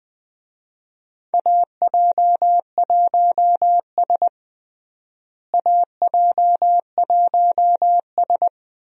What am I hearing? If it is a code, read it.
Morse code, heard as a keyed tone, about 20 wpm, 700 Hz. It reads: AJ1S AJ1S
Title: USA Call Signs - 4 Characters - Repeated 2x - 20wpm